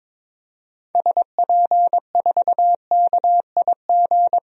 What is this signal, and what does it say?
Morse code, heard as a keyed tone, about 22 wpm, 700 Hz. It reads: SP4KIG
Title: DXCC-100 Call Signs - 6 Characters - 22wpm